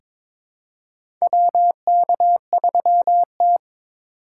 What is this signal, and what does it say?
Morse code, heard as a keyed tone, about 22 wpm, 700 Hz. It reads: WK3T
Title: USA Call Signs - 4 Characters - 2x Character Spacing - 22wpm